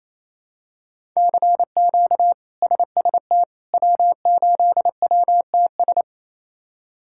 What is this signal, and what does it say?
Morse code, heard as a keyed tone, about 27 wpm, 700 Hz. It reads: CQ SST W8WTS
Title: K1USN SST QSO Practice - 28wpm